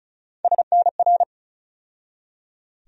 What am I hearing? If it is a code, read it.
Morse code, heard as a keyed tone, about 36 wpm, 700 Hz. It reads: SNR
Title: USA Call Sign Postfixes - 3 Letters 35wpm